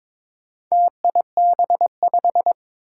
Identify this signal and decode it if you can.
Morse code, heard as a keyed tone, about 22 wpm, 700 Hz. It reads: TIB5
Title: Four Challenging Letter-Number Combinations 22wpm